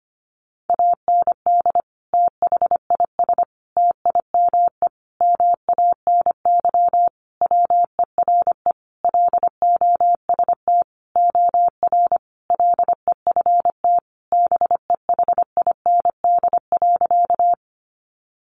Morse code, heard as a keyed tone, about 25 wpm, 700 Hz. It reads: AND THIS TIME MANY WERE LOST OR LEFT BEHIND.
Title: Sentences from Top 1400 Words 25wpm